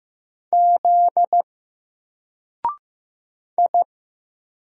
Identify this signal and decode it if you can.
Morse code, heard as a keyed tone, about 15 wpm, 700 Hz. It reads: Z I
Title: Single Letter-Number Alt 15wpm